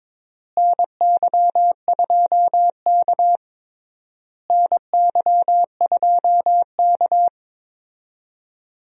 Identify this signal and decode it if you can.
Morse code, heard as a keyed tone, about 21 wpm, 700 Hz. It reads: NY2K NY2K
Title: USA Call Signs - 4 Characters - Repeated 3x - 22wpm